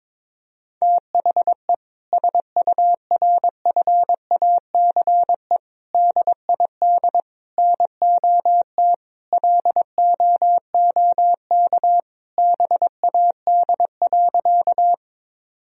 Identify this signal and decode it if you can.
Morse code, heard as a keyed tone, about 22 wpm, 700 Hz. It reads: THE SURFACE DID NOT LOOK BAD.
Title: Sentences from Top 1000 Words - Review Part 3 - 22wpm